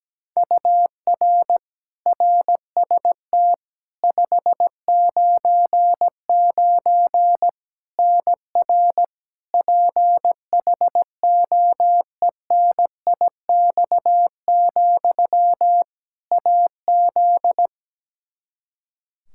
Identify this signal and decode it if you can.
Morse code, heard as a keyed tone, about 17 wpm, 700 Hz. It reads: UR RST 599 NR PHOENIX, AZ
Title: Practice QSO - Exchange 1 - Beginning - 17wpm